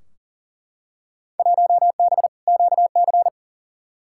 Morse code, heard as a keyed tone, about 40 wpm, 700 Hz. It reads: 1B QC
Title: Field Day Exchanges - 40wpm